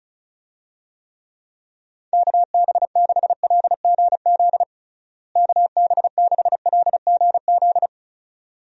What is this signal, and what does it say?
Morse code, heard as a keyed tone, about 34 wpm, 700 Hz. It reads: KB6LGZ KB6LGZ
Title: USA Call Signs - 6 Characters - Repeated 2x - 35wpm